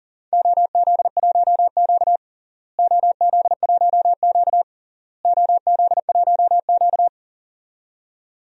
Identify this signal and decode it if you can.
Morse code, heard as a keyed tone, about 40 wpm, 700 Hz. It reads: OZ1Q OZ1Q OZ1Q
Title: DXCC-100 Call Signs - 4 Characters - Repeated 3x - 40wpm